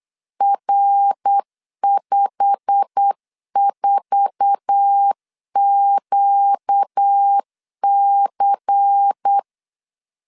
Morse code, heard as a keyed tone, about 8 wpm, 800 Hz. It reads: R54QC
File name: LEZIONI / 8 YJ45 / 8 YJ45 8wpm 3214